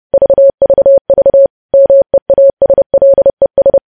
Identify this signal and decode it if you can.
Morse code, heard as a keyed tone, about 30 wpm, 550 Hz. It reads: VVV MEASLES
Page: news headlines in morse code at 30 wpm